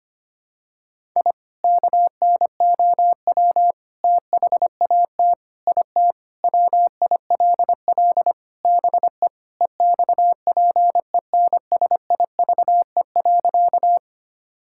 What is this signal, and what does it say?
Morse code, heard as a keyed tone, about 25 wpm, 700 Hz. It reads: I KNOW THAT IT WILL BE EXPENSIVE.
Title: Sentences from Top 1000 Words - Review Part 6 - 25wpm